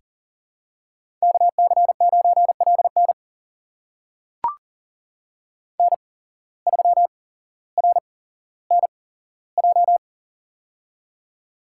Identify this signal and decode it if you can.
Morse code, heard as a keyed tone, about 40 wpm, spaced as for 17 wpm, 700 Hz. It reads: KC9LN N3RNJ
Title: USA Call Signs - 5 Characters - 8x Character Spacing - 40wpm